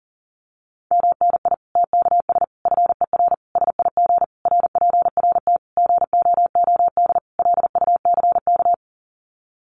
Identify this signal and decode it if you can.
Morse code, heard as a keyed tone, about 40 wpm, 700 Hz. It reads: MNI TKS FER SIG RPRT GOOD LUCK
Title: Common QSO Phrases 40wpm